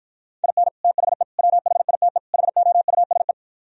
Morse code, heard as a keyed tone, about 53 wpm, 700 Hz. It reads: IN THE WHITE HOUSE